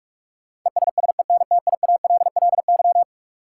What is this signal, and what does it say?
Morse code, heard as a keyed tone, about 47 wpm, 700 Hz. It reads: ESSENTIALLY
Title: Sentences from Top 1300 Words - New Word Drill Alt - 45wpm